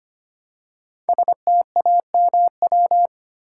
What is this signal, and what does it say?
Morse code, heard as a keyed tone, about 25 wpm, 700 Hz. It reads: STAMW
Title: Code Groups - 5 Letters - 25wpm